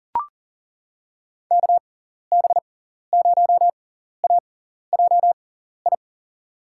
Morse code, heard as a keyed tone, about 40 wpm, spaced as for 14 wpm, 700 Hz. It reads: KB0AJI
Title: USA Call Signs - 6 Characters - 6x Character Spacing - 40wpm